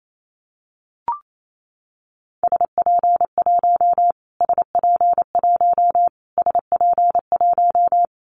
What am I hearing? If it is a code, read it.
Morse code, heard as a keyed tone, about 28 wpm, 700 Hz. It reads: SP1 SP1 SP1
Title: Callsign Prefixes for Top 25 DXCC - Repeated 3x - 28wpm